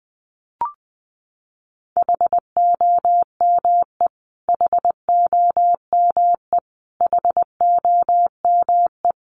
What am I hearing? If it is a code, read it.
Morse code, heard as a keyed tone, about 20 wpm, 700 Hz. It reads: HOME HOME HOME
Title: Top 500 Words Repeated 20wpm